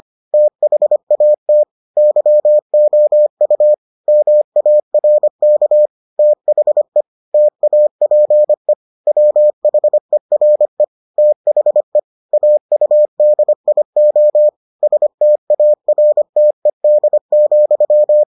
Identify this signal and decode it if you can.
Morse code, heard as a keyed tone, about 25 wpm, 600 Hz. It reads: THAT YOU MARK THE TAPE WHERE THE AUDIO STARTED,